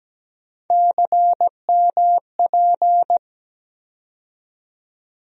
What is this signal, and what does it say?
Morse code, heard as a keyed tone, about 17 wpm, 700 Hz. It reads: CMP
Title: USA Call Sign Postfixes - 3 Letters 17wpm